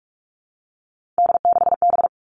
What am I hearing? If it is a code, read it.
Morse code, heard as a keyed tone, about 46 wpm, 700 Hz. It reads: D6B
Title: Three Challenging Letter-Number Combinations 45wpm